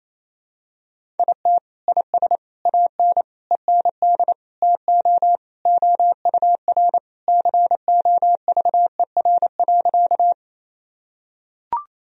Morse code, heard as a keyed tone, about 27 wpm, 700 Hz. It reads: IT IS AN END TO OUR COVER.
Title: Sentences from Top 1000 Words - Review Part 1 - 28wpm